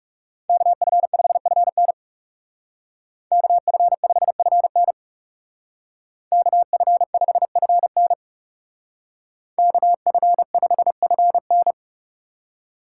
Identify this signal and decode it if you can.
Morse code, heard as a keyed tone, about 39 wpm, 700 Hz. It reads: KF5FN KF5FN KF5FN KF5FN